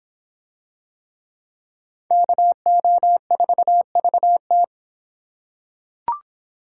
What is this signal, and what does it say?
Morse code, heard as a keyed tone, about 25 wpm, 700 Hz. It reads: KO4VT